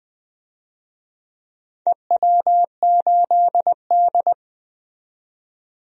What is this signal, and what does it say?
Morse code, heard as a keyed tone, about 20 wpm, 700 Hz. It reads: EW8D
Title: DXCC-100 Call Signs - 4 Characters - 20wpm